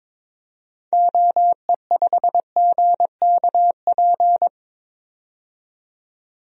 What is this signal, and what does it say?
Morse code, heard as a keyed tone, about 22 wpm, 700 Hz. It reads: OE5GKP
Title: DXCC-100 Call Signs - 6 Characters - 22wpm